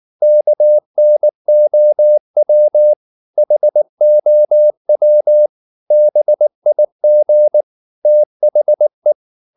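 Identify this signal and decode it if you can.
Morse code, heard as a keyed tone, about 19 wpm, 600 Hz. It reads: KNOW HOW BIG THE
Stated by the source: Morse code at 19 wpm from CW med Gnister, reading Ken Follett's novel Never in English.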